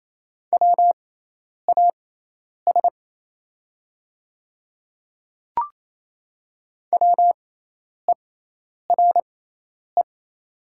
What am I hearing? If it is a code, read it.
Morse code, heard as a keyed tone, about 28 wpm, spaced as for 10 wpm, 700 Hz. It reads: WAS WERE
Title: 6x Character Spacing - Top 100 Words - 28wpm